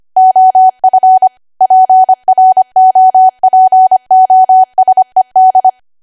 Morse code, heard as a keyed tone, about 25 wpm, 750 Hz. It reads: OF PROPOSED